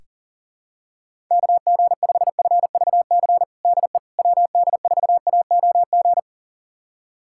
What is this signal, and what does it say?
Morse code, heard as a keyed tone, about 40 wpm, 700 Hz. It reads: KG5FVC DE WD4AOG
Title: USA Callsign DE Callsign 40wpm